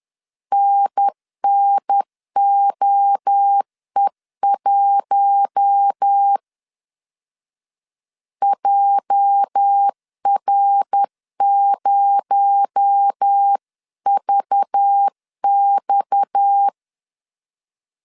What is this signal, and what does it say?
Morse code, heard as a keyed tone, about 10 wpm, 800 Hz. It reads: NNOE1 JR0VX